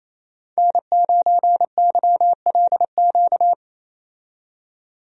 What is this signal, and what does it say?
Morse code, heard as a keyed tone, about 27 wpm, 700 Hz. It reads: N9YLQ
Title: USA Call Signs - 5 Characters - 28wpm